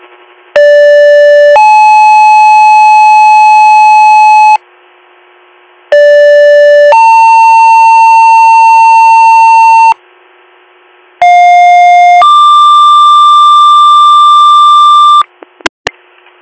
Two-tone page, heard as a one-second tone, then a three-second tone, about 580 then 860 Hz, about 580 then 900 Hz, about 710 then 1180 Hz, with noise. It is heard from 0.6 to 15.2 s.